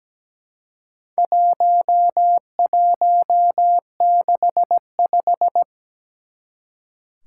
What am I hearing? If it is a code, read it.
Morse code, heard as a keyed tone, about 17 wpm, 700 Hz. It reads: 1165
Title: Four Challenging Letter-Number Combinations 17wpm